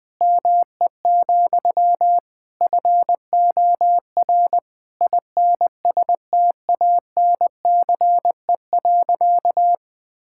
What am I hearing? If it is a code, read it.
Morse code, heard as a keyed tone, about 21 wpm, 700 Hz. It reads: ME, FOR INSTANCE.
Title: Sentences from Top 1000 Words - Review Part 5 - 20wpm